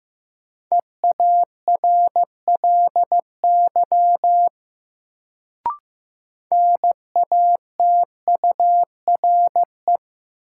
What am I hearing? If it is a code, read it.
Morse code, heard as a keyed tone, about 15 wpm, 700 Hz. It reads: EARLY NATURE
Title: Sets of 3 Words - New Word Drill - 15wpm